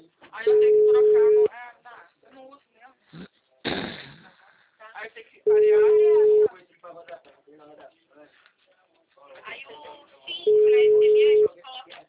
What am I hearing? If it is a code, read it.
Ringback tone (Europe, 425 Hz), with speech.
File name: out- -2006- - - .30